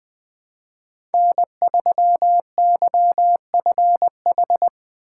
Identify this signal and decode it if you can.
Morse code, heard as a keyed tone, about 20 wpm, 700 Hz. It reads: N3YFH
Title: USA Call Signs - 5 Characters - 8x Character Spacing - 20wpm